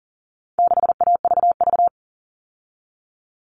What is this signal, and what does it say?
Morse code, heard as a keyed tone, about 40 wpm, 700 Hz. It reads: 6AVV
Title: All Cut Numbers - 4-Digit 40wpm